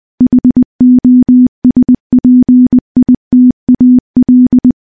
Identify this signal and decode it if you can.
Morse code, heard as a keyed tone, about 21 wpm, 250 Hz. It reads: HOSPITAL
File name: morse / morse-training-20WPM